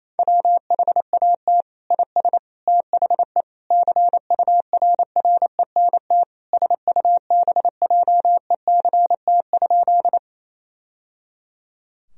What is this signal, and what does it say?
Morse code, heard as a keyed tone, about 28 wpm, 700 Hz. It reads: WHAT IS THE CURRENT SUBJECT?
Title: Sentences from Top 1000 Words - Review Part 4 - 28wpm